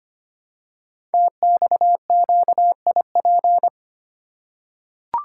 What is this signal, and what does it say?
Morse code, heard as a keyed tone, about 25 wpm, 700 Hz. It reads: TXQIP
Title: Code Groups - 5 Letters - 25wpm